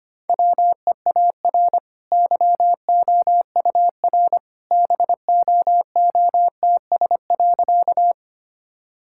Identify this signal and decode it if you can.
Morse code, heard as a keyed tone, about 25 wpm, 700 Hz. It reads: WEAR YOUR BOOTS.
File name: Easy Sentences with Hints - 25wpm